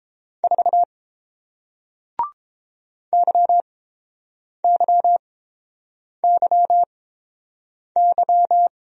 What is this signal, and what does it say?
Morse code, heard as a keyed tone, about 30 wpm, spaced as for 7 wpm, 700 Hz. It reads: 4 YYYY